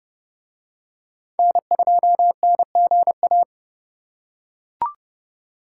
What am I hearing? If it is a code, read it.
Morse code, heard as a keyed tone, about 30 wpm, 700 Hz. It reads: N2NGA